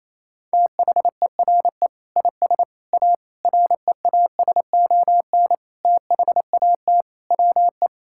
Morse code, heard as a keyed tone, about 28 wpm, 700 Hz. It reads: THERE IS A REASON THAT WE
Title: Sentences from Top 1000 Words - Review Part 5 - 28wpm